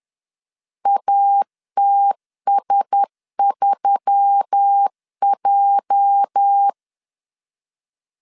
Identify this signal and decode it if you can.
Morse code, heard as a keyed tone, about 10 wpm, 800 Hz. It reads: ATS3J
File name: LEZIONI / 8 YJ45 / 8 YJ 10wpm 2651